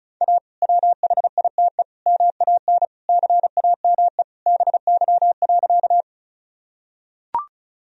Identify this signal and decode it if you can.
Morse code, heard as a keyed tone, about 36 wpm, 700 Hz. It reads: A WHITE MAN CAME BY.
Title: Sentences from Top 1000 Words - Review Part 1 - 35wpm